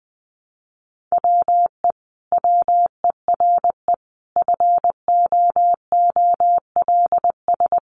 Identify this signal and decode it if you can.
Morse code, heard as a keyed tone, about 19 wpm, 700 Hz. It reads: WE WERE FOOLS
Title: A Sound Of Thunder 20wpm